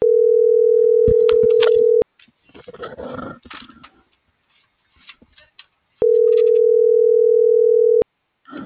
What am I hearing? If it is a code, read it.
Ringback tone (North America), 440 + 480 Hz.